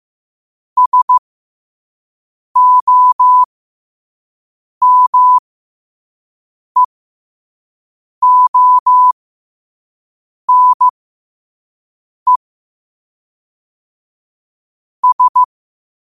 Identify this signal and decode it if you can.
Morse code, heard as a keyed tone, about 15 wpm, spaced as for 5 wpm, 1000 Hz. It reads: SOMEONE S
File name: qotdfiles / 5wpm